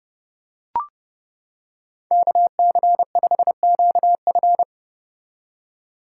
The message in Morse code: KC5QF

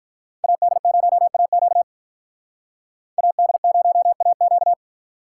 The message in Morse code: AD0AQ AD0AQ

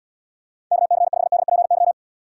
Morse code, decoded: KC5DPY

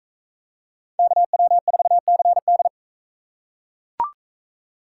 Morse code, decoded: KW4CD

